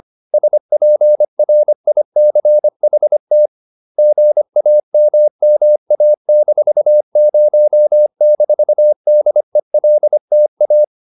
Morse code, heard as a keyed tone, about 25 wpm, 600 Hz. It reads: SPRICHT GAMMA-0-DELTA